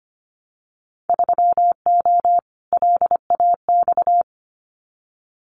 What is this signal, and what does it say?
Morse code, heard as a keyed tone, about 25 wpm, 700 Hz. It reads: 3O LAX
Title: Winter Field Day Exchanges Encore 25wpm